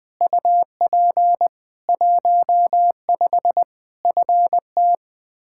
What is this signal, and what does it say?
Morse code, heard as a keyed tone, about 20 wpm, 700 Hz. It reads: UP 15 FT
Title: K1USN SST QSO Practice 20wpm